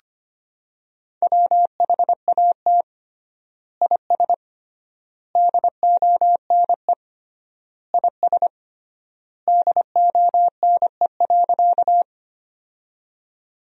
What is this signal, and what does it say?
Morse code, heard as a keyed tone, about 25 wpm, 700 Hz. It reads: WHAT IS DONE IS DONE.
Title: 3x Word-Spacing - Sentences from Top 500 Words - 25wpm